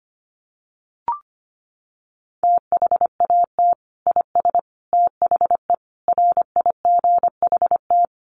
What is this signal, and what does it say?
Morse code, heard as a keyed tone, about 25 wpm, 700 Hz. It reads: THAT IS THE RIGHT
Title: Sentences from Top 1400 Words 25wpm